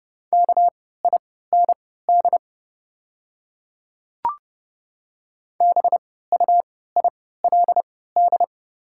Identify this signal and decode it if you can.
Morse code, heard as a keyed tone, about 29 wpm, spaced as for 17 wpm, 690 Hz. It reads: KIND BUILD
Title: 3x Character Spacing - Top 200 Words - 30wpm